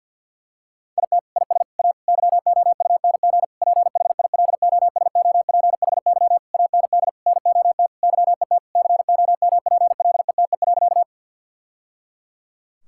Morse code, heard as a keyed tone, about 50 wpm, 700 Hz. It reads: IT IS A YOUNG PHILOSOPHY AND NOT YET COMPLETE.
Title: Sentences from Top 1000 Words - Review Part 6 - 50wpm